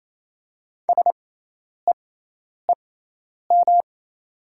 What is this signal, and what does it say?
Morse code, heard as a keyed tone, about 28 wpm, spaced as for 10 wpm, 700 Hz. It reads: SEEM
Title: 6x Character Spacing - Top 300 Words - 28wpm